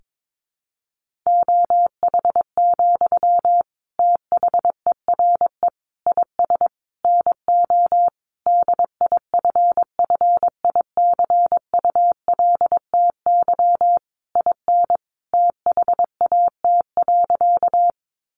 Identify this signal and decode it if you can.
Morse code, heard as a keyed tone, about 22 wpm, 700 Hz. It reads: OH, THERE IS NO DIFFICULTY IN THAT.
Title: Sentences from Top 1500 Words 22wpm